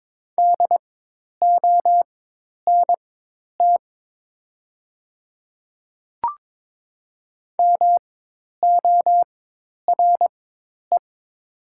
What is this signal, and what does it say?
Morse code, heard as a keyed tone, about 22 wpm, spaced as for 10 wpm, 700 Hz. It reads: DONT MORE
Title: 4x Character Spacing - Top 300 Words - 22wpm